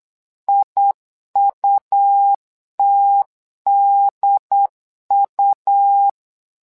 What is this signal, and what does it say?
Morse code, heard as a keyed tone, about 8 wpm, 800 Hz. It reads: IUTDU
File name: LEZIONI / 2 UHDN / 2 UHD 8wpm 2258